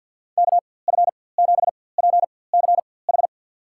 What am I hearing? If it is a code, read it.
Morse code, heard as a keyed tone, about 53 wpm, spaced as for 24 wpm, 700 Hz. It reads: KF7PCH
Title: USA Call Signs - 4x Character Spacing - 50wpm